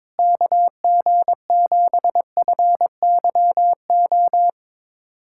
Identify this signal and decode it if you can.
Morse code, heard as a keyed tone, about 21 wpm, 700 Hz. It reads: KG7FYO
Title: USA Callsign DE Callsign 22wpm